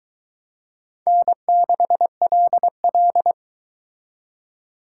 Morse code, heard as a keyed tone, about 22 wpm, 700 Hz. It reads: N6LL